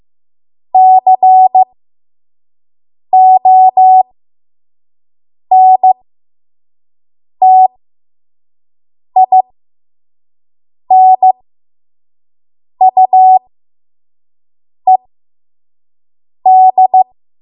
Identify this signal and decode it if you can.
Morse code, heard as a keyed tone, about 15 wpm, spaced as for 5 wpm, 750 Hz. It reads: CONTINUED
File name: Archive / 5 WPM / 05WPM